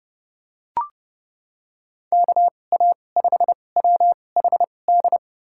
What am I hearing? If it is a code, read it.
Morse code, heard as a keyed tone, about 30 wpm, spaced as for 22 wpm, 700 Hz. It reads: KA5WHD
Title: USA Call Signs - 6 Characters - 2x Character Spacing - 30wpm